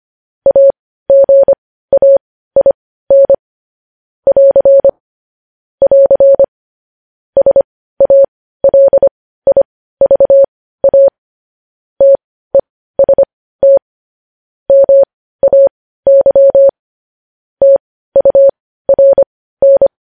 Morse code, heard as a keyed tone, about 25 wpm, spaced as for 15 wpm, 550 Hz. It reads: AGAIN <AR> <AR> SALIVA TEST MAY TURN
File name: podcast / News headlines-1x-25x15